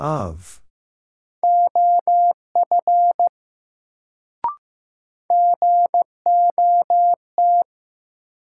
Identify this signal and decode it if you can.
Morse code, heard as a keyed tone, about 15 wpm, 700 Hz. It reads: OF GOT
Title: Sets of 2 Words - New Word Drill - 15wpm